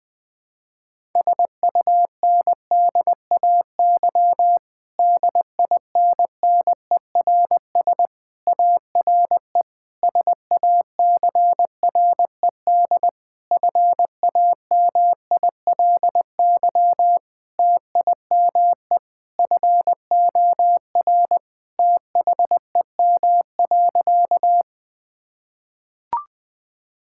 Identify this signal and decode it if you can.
Morse code, heard as a keyed tone, about 20 wpm, 700 Hz. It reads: SUNDAY DINNERS ARE SACRED FAMILY TIME FOR THEM.
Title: Difficult Sentences with Hints - 20wpm